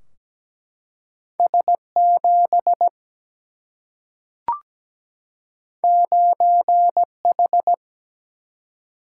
Morse code, heard as a keyed tone, about 17 wpm, 700 Hz. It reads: S 7 9 H
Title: Single-Letter Single-Number 17wpm